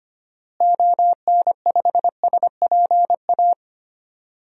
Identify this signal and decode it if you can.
Morse code, heard as a keyed tone, about 25 wpm, 700 Hz. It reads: ON5SPA
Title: DXCC-100 Call Signs - 6 Characters - Repeated 2x - 25wpm